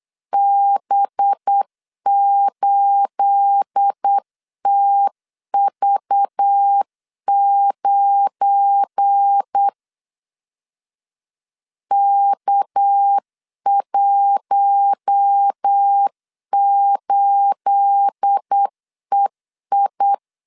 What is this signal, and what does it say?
Morse code, heard as a keyed tone, about 8 wpm, 800 Hz. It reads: B8TV9 K18EI
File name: LEZIONI / 9 28 8wpm 3512